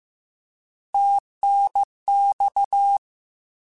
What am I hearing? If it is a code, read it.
Morse code, heard as a keyed tone, about 15 wpm, 800 Hz. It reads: TNX
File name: Abreviations / Abreviations mp3s / Abreviations@13